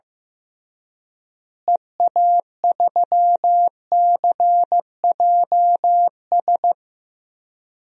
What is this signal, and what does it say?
Morse code, heard as a keyed tone, about 15 wpm, 700 Hz. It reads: EA3CJS
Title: DXCC-100 Call Signs - 6 Characters - Repeated 2x - 15wpm